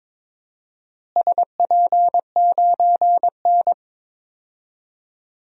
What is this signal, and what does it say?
Morse code, heard as a keyed tone, about 21 wpm, 700 Hz. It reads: SP9N
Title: DXCC-100 Call Signs - 4 Characters - Repeated 3x - 22wpm